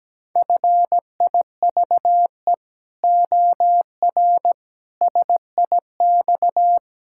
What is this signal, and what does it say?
Morse code, heard as a keyed tone, about 17 wpm, 700 Hz. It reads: FIVE OR SIX
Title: Pangram Challenge v2 - 17wpm